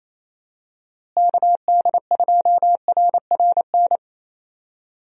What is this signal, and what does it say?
Morse code, heard as a keyed tone, about 28 wpm, 700 Hz. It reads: KD2RRN
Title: USA Call Signs - 6 Characters - Repeated 3x - 28wpm